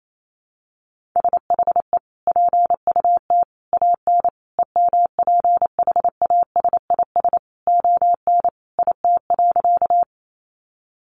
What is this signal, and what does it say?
Morse code, heard as a keyed tone, about 28 wpm, 700 Hz. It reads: SHE PUT AN EMPHASIS ON IT.
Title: Sentences from Top 1400 Words 28wpm